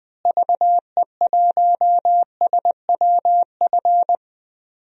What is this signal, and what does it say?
Morse code, heard as a keyed tone, about 20 wpm, 700 Hz. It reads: VE1SWF
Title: DXCC-100 Call Signs - 6 Characters - Repeated 2x - 20wpm